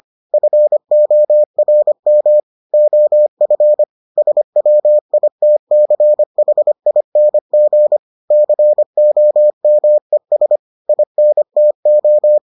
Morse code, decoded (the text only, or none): FORM OF SWITCHING COMES INTO